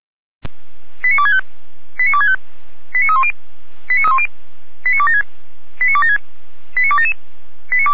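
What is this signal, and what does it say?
Five-tone selective call (ZVEI), heard as quick runs of five tones, tones of about 70 ms. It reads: 89265 89265 89310 89276 89276 89277
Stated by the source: ZVEI five-tone selective call received by SDR, fire brigade Feuerwehr Orlamünde.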